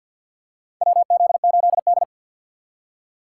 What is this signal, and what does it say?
Morse code, heard as a keyed tone, about 51 wpm, 700 Hz. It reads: WZ8D
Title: USA Call Signs - 4 Characters - 2x Character Spacing - 50wpm